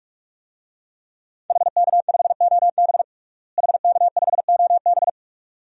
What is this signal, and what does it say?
Morse code, heard as a keyed tone, about 41 wpm, 700 Hz. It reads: HK5OB HK5OB
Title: DXCC-100 Call Signs - 5 Characters - Repeated 2x - 45wpm